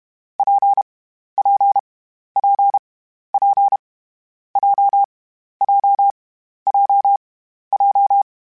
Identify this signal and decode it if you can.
Morse code, heard as a keyed tone, about 32 wpm, 800 Hz. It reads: P P P P J J J J
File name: Variable Speed / Variable Speed-mp3s / WPJ WPJ1 sp @30